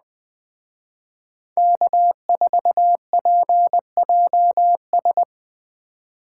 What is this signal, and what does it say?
Morse code, heard as a keyed tone, about 20 wpm, 700 Hz. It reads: K4PJS
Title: Speed Racing - USA 1x3 Call Signs 17wpm